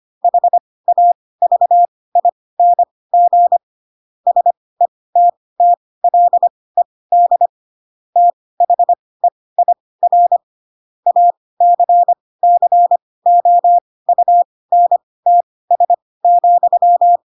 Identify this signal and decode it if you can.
Morse code, heard as a keyed tone, about 25 wpm, spaced as for 18 wpm, 700 Hz. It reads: HAVING SETTLED THEIR ACCOUNTS,
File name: Pinocchio / Pinocchio ch2 12 25 ik1hge